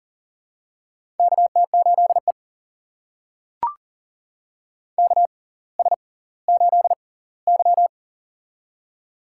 Morse code, heard as a keyed tone, about 40 wpm, spaced as for 21 wpm, 700 Hz. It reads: KT8E KS8Y
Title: USA Call Signs - 4 Characters - 6x Character Spacing - 40wpm